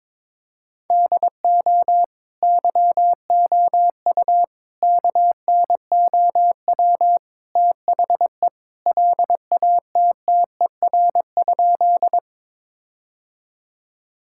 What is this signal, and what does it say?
Morse code, heard as a keyed tone, about 22 wpm, 700 Hz. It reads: DO YOU KNOW THE LATTER?
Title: Sentences from Top 1600 Words 22wpm